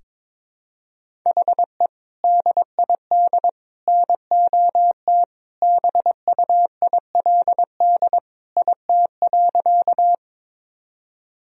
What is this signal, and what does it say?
Morse code, heard as a keyed tone, about 22 wpm, 700 Hz. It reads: HE DID NOT BUILD IT.
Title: Sentences from Top 1000 Words - Review Part 1 - 22wpm